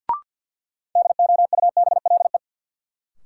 Morse code, heard as a keyed tone, about 48 wpm, 700 Hz. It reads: DOUBLE
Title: Sentences from Top 1400 Words - New Word Drill Alt - 50wpm